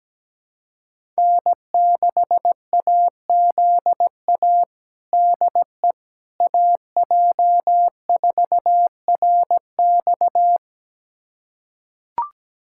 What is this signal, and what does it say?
Morse code, heard as a keyed tone, about 17 wpm, 700 Hz. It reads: N6AZA DE AJ4RX
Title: USA Callsign DE Callsign - 5 Characters - 17wpm